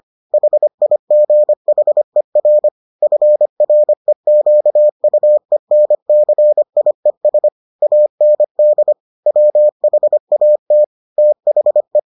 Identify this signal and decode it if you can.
Morse code, heard as a keyed tone, about 25 wpm, 600 Hz. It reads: HIGHER FREQUENCIES AND WHAT THE